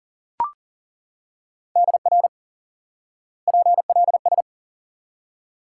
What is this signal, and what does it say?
Morse code, heard as a keyed tone, about 40 wpm, 700 Hz. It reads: DR PLS
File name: IQE Course 3 - Lesson 89 - YAGI - Two Elements - 40wpm